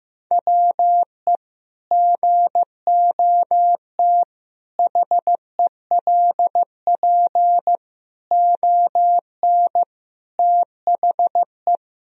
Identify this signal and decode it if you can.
Morse code, heard as a keyed tone, about 16 wpm, 700 Hz. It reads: WE GOT HELP ON THE
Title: Sentences from Top 1000 Words - Review Part 2 - 15wpm